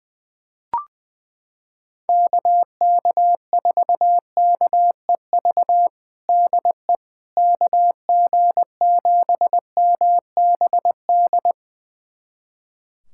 Morse code, heard as a keyed tone, about 21 wpm, 700 Hz. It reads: KK4KEV DE KG7MBD
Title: USA Callsign DE Callsign - 6 Characters - 20wpm